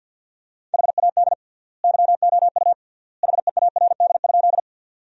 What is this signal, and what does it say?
Morse code, heard as a keyed tone, about 53 wpm, 700 Hz. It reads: HAD YOU HEARD?
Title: 3x Word-Spacing - Sentences from Top 400 Words - 50wpm